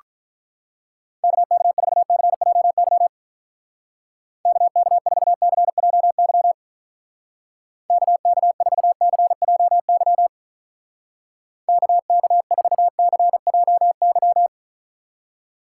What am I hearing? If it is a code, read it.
Morse code, heard as a keyed tone, about 45 wpm, 700 Hz. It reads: KK4CJY KK4CJY KK4CJY KK4CJY